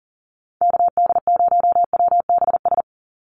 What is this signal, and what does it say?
Morse code, heard as a keyed tone, about 40 wpm, 700 Hz. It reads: KD0WBS
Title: USA Call Signs 40wpm